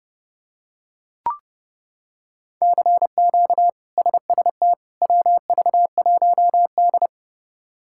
Morse code, heard as a keyed tone, about 30 wpm, 700 Hz. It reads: CQ SST WV1D
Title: K1USN SST QSO Practice - 30wpm